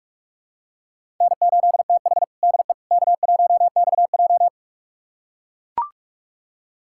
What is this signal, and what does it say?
Morse code, heard as a keyed tone, about 44 wpm, 700 Hz. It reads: N8TH DE K1XJ
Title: USA Callsign DE Callsign - 4 Characters - 45wpm